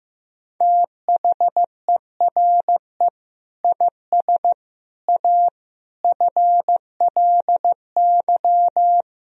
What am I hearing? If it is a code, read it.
Morse code, heard as a keyed tone, about 16 wpm, 700 Hz. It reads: THERE IS A FLY
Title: Sentences from Top 1000 Words - Review Part 4 - 15wpm